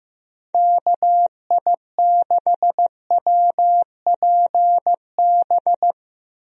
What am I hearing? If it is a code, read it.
Morse code, heard as a keyed tone, about 15 wpm, 700 Hz. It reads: KI6WPB
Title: USA Call Signs - 6 Characters - Repeated 3x - 15wpm